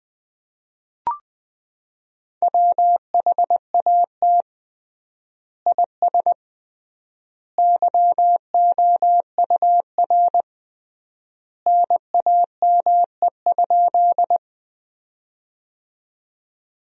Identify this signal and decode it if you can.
Morse code, heard as a keyed tone, about 20 wpm, 700 Hz. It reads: WHAT IS YOUR NAME?
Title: 3x Word-Spacing - Sentences from Top 200 Words - 20wpm